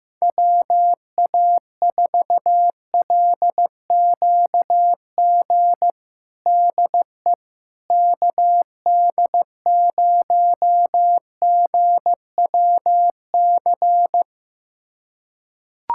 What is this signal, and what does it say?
Morse code, heard as a keyed tone, about 15 wpm, 700 Hz. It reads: WA4LQG DE KD0GWC
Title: USA Callsign DE Callsign - 6 Characters - 15wpm